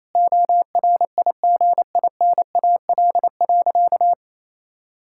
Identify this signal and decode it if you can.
Morse code, heard as a keyed tone, about 28 wpm, 700 Hz. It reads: ORIGINAL.
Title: Sentences from Top 1000 Words - Review Part 6 - 28wpm